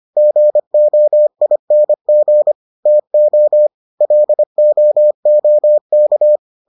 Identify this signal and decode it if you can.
Morse code, heard as a keyed tone, about 25 wpm, 600 Hz. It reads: GOING TO LOOK